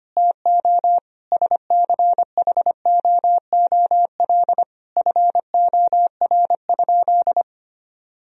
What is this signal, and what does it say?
Morse code, heard as a keyed tone, about 25 wpm, 700 Hz. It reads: TO SCHOOL FOR?
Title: Sentences from Top 1000 Words - Review Part 1 - 25wpm